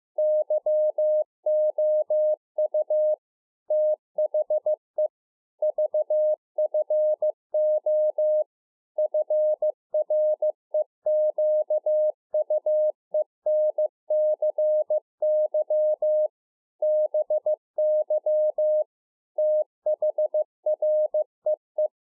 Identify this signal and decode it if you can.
Morse code, heard as a keyed tone, about 15 wpm, 610 Hz. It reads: YOU THE VFO FREQUENCY BY THREE
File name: 2011-04-mp3 / A SMALL TRANCEIVER 2 15WPM